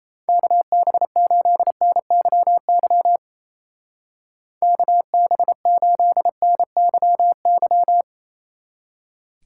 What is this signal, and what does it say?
Morse code, heard as a keyed tone, about 33 wpm, 700 Hz. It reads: KB8NYY KB8NYY
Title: Speed Racing - USA 2x3 Call Signs 28wpm